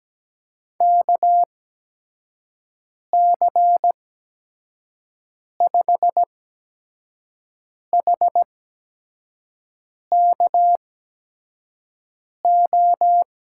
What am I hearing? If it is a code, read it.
Morse code, heard as a keyed tone, about 17 wpm, spaced as for 5 wpm, 700 Hz. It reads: KC5HKO
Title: USA Call Signs - 6 Characters - 8x Character Spacing - 17wpm